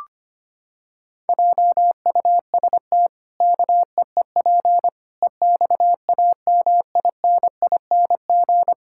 Morse code, heard as a keyed tone, about 25 wpm, 700 Hz. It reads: JUST KEEP EXAMINING
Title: Pangram Challenge v2 - 25wpm